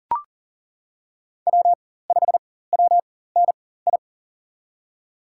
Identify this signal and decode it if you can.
Morse code, heard as a keyed tone, about 40 wpm, spaced as for 19 wpm, 700 Hz. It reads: W5WNI
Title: USA Call Signs - 5 Characters - 4x Character Spacing - 40wpm